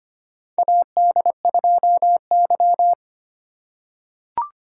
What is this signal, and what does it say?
Morse code, heard as a keyed tone, about 25 wpm, 700 Hz. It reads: AD2Y
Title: USA Call Signs - 4 Characters - 3x Character Spacing - 25wpm